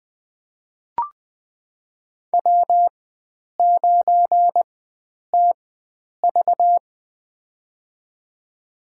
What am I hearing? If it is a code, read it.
Morse code, heard as a keyed tone, about 20 wpm, spaced as for 9 wpm, 700 Hz. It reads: W9TV
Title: USA Call Signs - 4 Characters - 4x Character Spacing - 20wpm